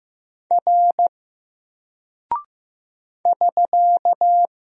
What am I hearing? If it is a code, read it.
Morse code, heard as a keyed tone, about 15 wpm, 700 Hz. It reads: R <SK>
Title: Single Character (Letter:Number:Punctuation:Prosign) 15wpm